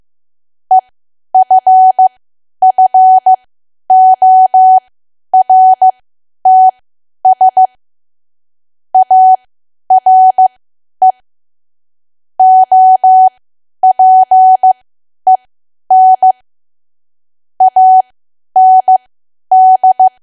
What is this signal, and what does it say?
Morse code, heard as a keyed tone, about 15 wpm, spaced as for 10 wpm, 750 Hz. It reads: EFFORTS ARE OPEN AND
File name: cwa / W1AW-Level2 / 10WPM